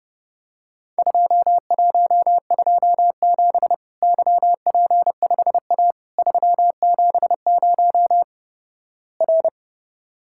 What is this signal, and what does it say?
Morse code, heard as a keyed tone, about 30 wpm, 700 Hz. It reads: 2127 YP5A 370 R
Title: Worked All Europe DX Contest - QTC Exchange - 30wpm